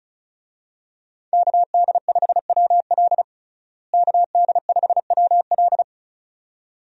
Morse code, heard as a keyed tone, about 34 wpm, 700 Hz. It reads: KD5WL KD5WL
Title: USA Call Signs - 5 Characters - Repeated 2x - 35wpm